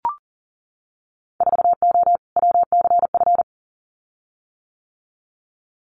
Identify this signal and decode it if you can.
Morse code, heard as a keyed tone, about 40 wpm, 700 Hz. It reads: 4O WCF
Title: Winter Field Day Exchanges Encore 40wpm